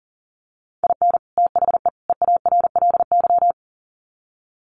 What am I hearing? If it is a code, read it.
Morse code, heard as a keyed tone, about 40 wpm, 700 Hz. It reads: IN THE EARLY
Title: Sets of 3 Words 40wpm